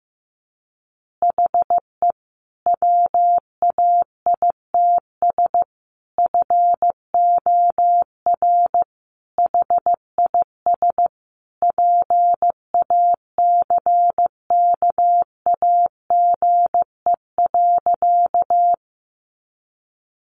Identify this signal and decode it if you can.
Morse code, heard as a keyed tone, about 14 wpm, 700 Hz. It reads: HE WAITS FOR HIS PACKAGE.
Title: Additional Easy Phrases 15wpm